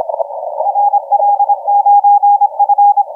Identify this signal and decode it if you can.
Morse code, heard as a keyed tone, about 26 wpm, 820 Hz, with radio noise. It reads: RL9F
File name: ZL7X 0406UTC 20m LP